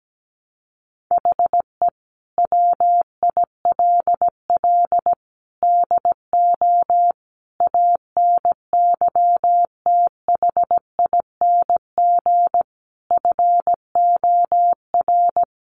Morse code, heard as a keyed tone, about 17 wpm, 700 Hz. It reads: HE WILL DO ANYTHING FOR
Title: Sentences from Top 1000 Words 17wpm